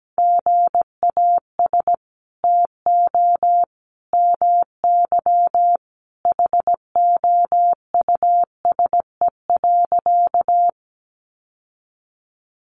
Morse code, heard as a keyed tone, about 17 wpm, 700 Hz. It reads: GAS TO MY HOUSE.
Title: Sentences from Top 1100 Words 17wpm